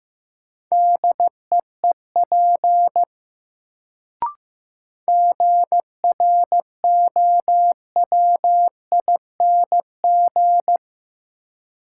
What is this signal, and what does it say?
Morse code, heard as a keyed tone, about 15 wpm, 700 Hz. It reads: DEEP GROWING
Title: Sets of 5 Words Encore - New Word Drill - 15wpm